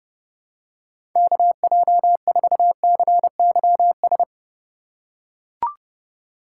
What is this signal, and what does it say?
Morse code, heard as a keyed tone, about 32 wpm, 700 Hz. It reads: KJ4CYS
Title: USA Call Signs - 4x Character Spacing - 30wpm